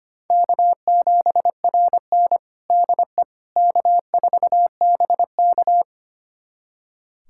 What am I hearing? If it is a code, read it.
Morse code, heard as a keyed tone, about 25 wpm, 700 Hz. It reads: K7RN DE K4BK
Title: USA Callsign DE Callsign - 4 Characters - 25wpm